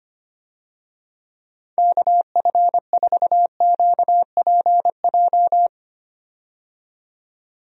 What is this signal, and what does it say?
Morse code, heard as a keyed tone, about 25 wpm, 700 Hz. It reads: KF4QPJ
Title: USA Call Signs - 6 Characters - 25wpm